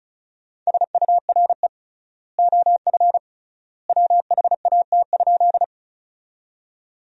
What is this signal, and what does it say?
Morse code, heard as a keyed tone, about 36 wpm, 700 Hz. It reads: SURE OF WHAT?
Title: 3x Word-Spacing - Sentences from Top 300 Words - 35wpm